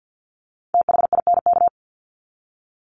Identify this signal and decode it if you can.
Morse code, heard as a keyed tone, about 50 wpm, 700 Hz. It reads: THINK
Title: Top 300 Words 50wpm